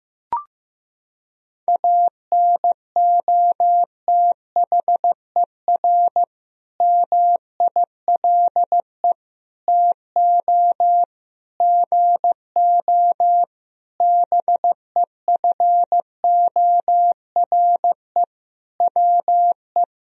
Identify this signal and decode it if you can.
Morse code, heard as a keyed tone, about 15 wpm, 700 Hz. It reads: ANOTHER MILE TO GO BEFORE WE
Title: Sentences from Top 1000 Words - Review Part 1 - 15wpm